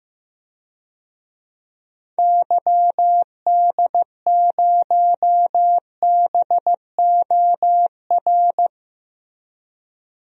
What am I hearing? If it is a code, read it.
Morse code, heard as a keyed tone, about 15 wpm, 700 Hz. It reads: YD0BOR
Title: DXCC-100 Call Signs - 6 Characters - 15wpm